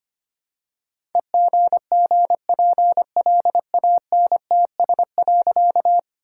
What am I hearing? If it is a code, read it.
Morse code, heard as a keyed tone, about 25 wpm, 700 Hz. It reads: EGGPLANTS.
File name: practice / 25wpm / All-Age Appropriate Jokes - 25wpm